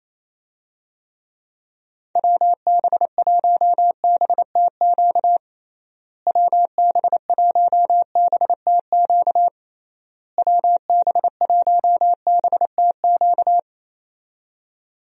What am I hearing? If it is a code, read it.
Morse code, heard as a keyed tone, about 28 wpm, 700 Hz. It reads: WB1BTQ WB1BTQ WB1BTQ
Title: USA Call Signs - 6 Characters - Repeated 3x - 28wpm